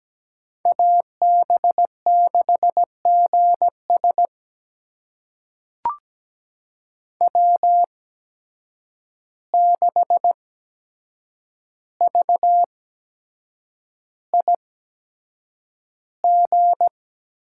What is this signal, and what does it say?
Morse code, heard as a keyed tone, about 17 wpm, spaced as for 7 wpm, 700 Hz. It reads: AB6GS W6VIG
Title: USA Call Signs - 5 Characters - 8x Character Spacing - 17wpm